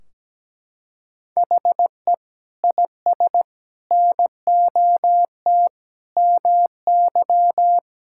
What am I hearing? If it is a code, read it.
Morse code, heard as a keyed tone, about 17 wpm, 700 Hz. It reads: HE IS NOT MY
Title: Sentences from Top 1600 Words 17wpm